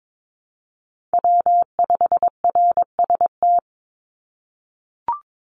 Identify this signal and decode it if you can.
Morse code, heard as a keyed tone, about 22 wpm, 700 Hz. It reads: W5RST
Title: NA SOTA Chaser Callsigns 22wpm